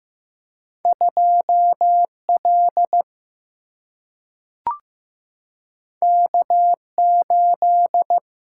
Morse code, heard as a keyed tone, about 15 wpm, 700 Hz. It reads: 2L K8
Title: Single-Letter Single-Number 15wpm